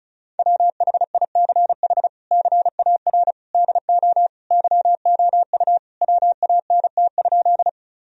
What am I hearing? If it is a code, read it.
Morse code, heard as a keyed tone, about 36 wpm, 700 Hz. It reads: WHICH CAR DO YOU WANT?
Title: Sentences from Top 1000 Words - Review Part 1 - 35wpm